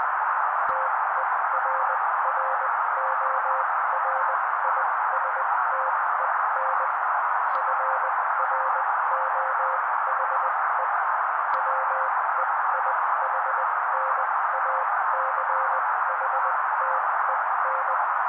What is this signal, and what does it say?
Morse code, heard as a keyed tone, about 20 wpm, spaced as for 16 wpm, 530 Hz. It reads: TERRORISTEN FROHE WEIHNACHTEN
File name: daily / 16wpm noise